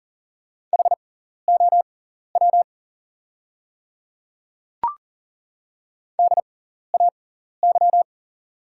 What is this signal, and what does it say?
Morse code, heard as a keyed tone, about 40 wpm, spaced as for 14 wpm, 690 Hz. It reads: HOW DAY
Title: 6x Character Spacing - Top 100 Words - 40wpm